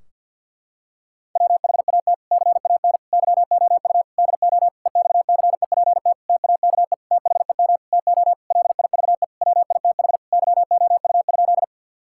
Morse code, heard as a keyed tone, about 51 wpm, 700 Hz. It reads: WHAT CAN YOU DO EXCEPT TAKE THEM TO LIVE WITH YOU?
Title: Sentences from Top 1600 Words 50wpm